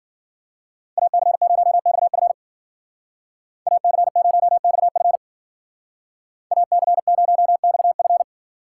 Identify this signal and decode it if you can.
Morse code, heard as a keyed tone, about 50 wpm, 700 Hz. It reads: AC0XF AC0XF AC0XF
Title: Speed Racing - USA 2x2 Call Signs 40wpm